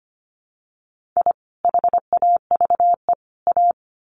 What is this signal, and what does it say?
Morse code, heard as a keyed tone, about 25 wpm, 700 Hz. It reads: I HAVE A
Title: Sets of 3 Words 25wpm